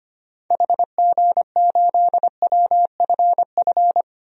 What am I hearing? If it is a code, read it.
Morse code, heard as a keyed tone, about 25 wpm, 700 Hz. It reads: HG8WFF
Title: DXCC-100 Call Signs - 6 Characters - 25wpm